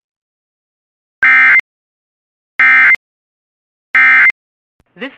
Emergency Alert System (SAME) tones, heard as harsh buzzing bursts. It is heard from 1.2 to 4.3 s.